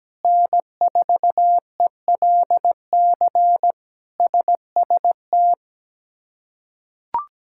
Morse code, heard as a keyed tone, about 17 wpm, 700 Hz. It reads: N4ELC SST
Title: K1USN SST QSO Practice - 17wpm